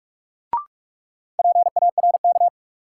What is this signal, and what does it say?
Morse code, heard as a keyed tone, about 46 wpm, 700 Hz. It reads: PARK
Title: Sentences from Top 1300 Words - New Word Drill Alt - 45wpm